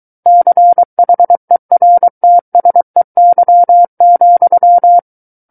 Morse code, heard as a keyed tone, about 23 wpm, 700 Hz. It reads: CHERTSEY,
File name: Book One / 14-Chapter 14 - 23 WPM